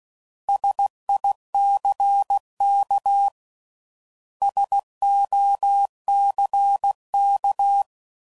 Morse code, heard as a keyed tone, about 16 wpm, 800 Hz. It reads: SICK SOCK
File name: Prefixes / By First Letter / Ss